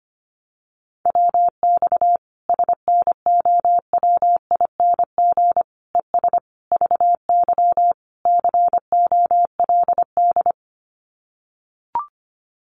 Morse code, heard as a keyed tone, about 25 wpm, 700 Hz. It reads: WX SNOWING ES VY COLD
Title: Common QSO Phrases 25wpm